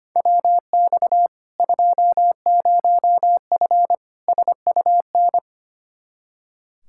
Morse code, heard as a keyed tone, about 25 wpm, 700 Hz. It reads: WX 20F SUN
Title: Practice QSO - Exchange 2 - Info - 25wpm